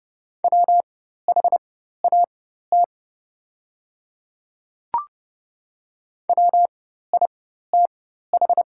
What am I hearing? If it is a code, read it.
Morse code, heard as a keyed tone, about 30 wpm, spaced as for 14 wpm, 700 Hz. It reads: WHAT WITH
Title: 4x Character Spacing - Top 100 Words - 30wpm